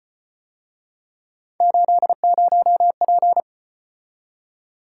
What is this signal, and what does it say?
Morse code, heard as a keyed tone, about 33 wpm, 700 Hz. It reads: 80P